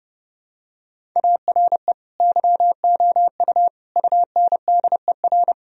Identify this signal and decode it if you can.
Morse code, heard as a keyed tone, about 30 wpm, 700 Hz. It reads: ARE YOU UNDER
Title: Sentences from Top 1000 Words - Review Part 1 - 30wpm